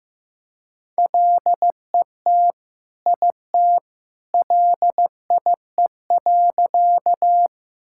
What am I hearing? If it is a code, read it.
Morse code, heard as a keyed tone, about 15 wpm, 700 Hz. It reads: LET IT LIE.
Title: Sentences from Top 1600 Words 15wpm